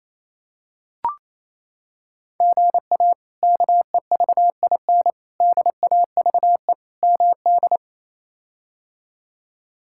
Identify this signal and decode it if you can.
Morse code, heard as a keyed tone, about 28 wpm, 700 Hz. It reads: GA KEVIN DAVE MD
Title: K1USN SST QSO Practice - 28wpm